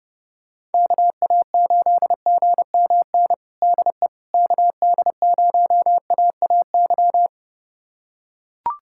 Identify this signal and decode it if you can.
Morse code, heard as a keyed tone, about 30 wpm, 700 Hz. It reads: KA8GMN DE KD0AAY
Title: USA Callsign DE Callsign 30wpm